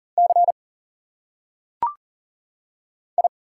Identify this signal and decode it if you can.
Morse code, heard as a keyed tone, about 39 wpm, 700 Hz. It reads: C I
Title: Single Letters - Phonetic Pronunciation 40wpm